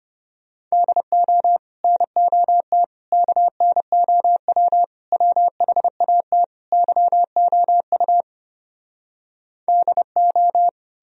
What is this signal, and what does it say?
Morse code, heard as a keyed tone, about 33 wpm, 700 Hz. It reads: DO NOT KNOW WHAT YOU DO